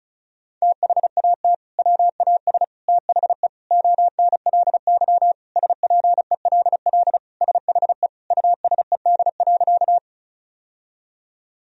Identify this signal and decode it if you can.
Morse code, heard as a keyed tone, about 34 wpm, 700 Hz. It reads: THAT WAS THE ONLY SPELL SHE USED.
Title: Sentences from Top 1000 Words - Review Part 1 - 35wpm